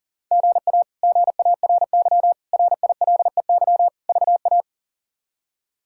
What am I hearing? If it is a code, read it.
Morse code, heard as a keyed tone, about 37 wpm, 700 Hz. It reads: GA GARY RILEY VA
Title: K1USN SST QSO Practice - 40wpm